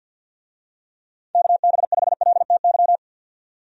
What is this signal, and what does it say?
Morse code, heard as a keyed tone, about 50 wpm, 700 Hz. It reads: KB5LTY
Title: USA Call Signs - 6 Characters - 50wpm